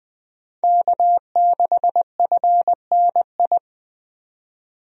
Morse code, heard as a keyed tone, about 20 wpm, 700 Hz. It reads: K6FNI
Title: USA Call Signs - 5 Characters - 20wpm